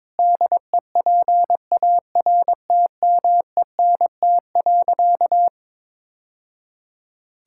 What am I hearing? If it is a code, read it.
Morse code, heard as a keyed tone, about 22 wpm, 700 Hz. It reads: DEPARTMENT.
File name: practice / 22wpm / Sentences with 3-Word Phrases - 22wpm